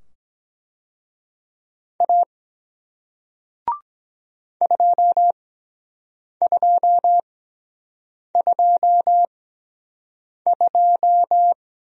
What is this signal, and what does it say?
Morse code, heard as a keyed tone, about 23 wpm, spaced as for 7 wpm, 700 Hz. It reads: A 2222